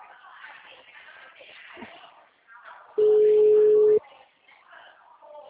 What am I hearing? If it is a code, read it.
Ringback tone (Europe, 425 Hz).